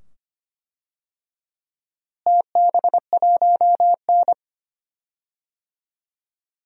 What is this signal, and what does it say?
Morse code, heard as a keyed tone, about 25 wpm, 700 Hz. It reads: TB1N
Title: DXCC-100 Call Signs - 4 Characters - Repeated 2x - 25wpm